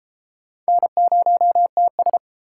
Morse code, heard as a keyed tone, about 33 wpm, 700 Hz. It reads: N0TS